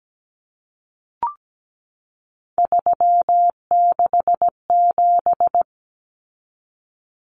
Morse code, heard as a keyed tone, about 17 wpm, 700 Hz. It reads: 367
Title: Common Cut Numbers - 3-Digit 17wpm